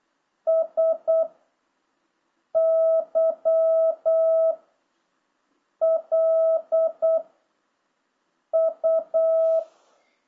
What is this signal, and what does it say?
Morse code, heard as a keyed tone, about 8 wpm, 650 Hz. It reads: S Y L U